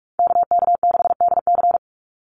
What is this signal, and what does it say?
Morse code, heard as a keyed tone, about 44 wpm, 700 Hz. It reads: KK6DC
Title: USA Call Signs 45wpm